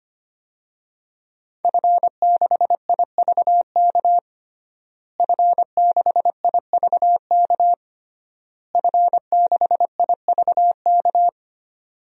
Morse code, heard as a keyed tone, about 25 wpm, 700 Hz. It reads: F6IVK F6IVK F6IVK
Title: DXCC-100 Call Signs - 5 Characters - Repeated 3x - 25wpm